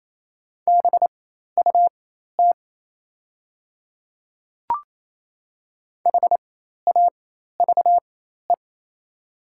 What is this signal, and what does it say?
Morse code, heard as a keyed tone, about 27 wpm, spaced as for 13 wpm, 690 Hz. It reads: BUT HAVE
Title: 4x Character Spacing - Top 100 Words - 28wpm